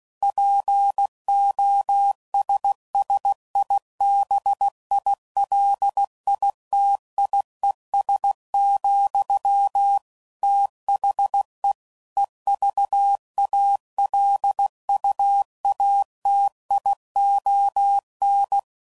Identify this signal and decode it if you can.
Morse code, heard as a keyed tone, about 16 wpm, 800 Hz. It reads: POSSIBILITIES, THE EVALUATION
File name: CW Code Groups @15WPM / Technical Subjects / AND Gate